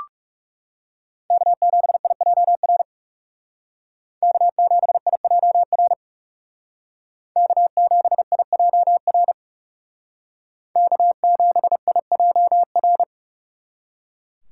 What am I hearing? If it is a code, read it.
Morse code, heard as a keyed tone, about 38 wpm, 700 Hz. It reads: K7IJR K7IJR K7IJR K7IJR